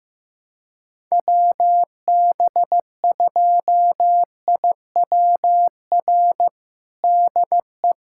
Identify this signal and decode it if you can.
Morse code, heard as a keyed tone, about 16 wpm, 700 Hz. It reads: WB2IWR DE
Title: USA Callsign DE Callsign 15wpm